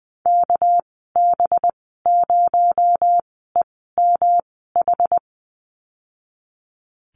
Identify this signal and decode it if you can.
Morse code, heard as a keyed tone, about 20 wpm, spaced as for 14 wpm, 700 Hz. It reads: KB0EMH
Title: 219 Lesson 35g - Number 0 - Call Signs 20wpm